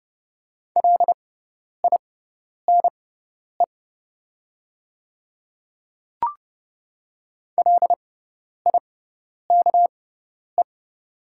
Morse code, heard as a keyed tone, about 30 wpm, spaced as for 10 wpm, 700 Hz. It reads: LINE LIKE
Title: 6x Character Spacing - Top 200 Words - 30wpm